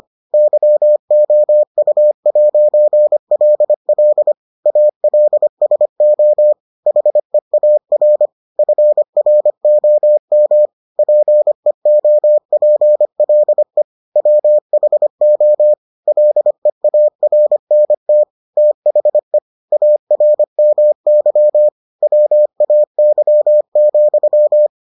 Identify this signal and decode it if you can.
Morse code, heard as a keyed tone, about 25 wpm, 600 Hz. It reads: YOU'LL ALSO HEAR FROM PEOPLE WHO LEARNT THE ARMY WAY,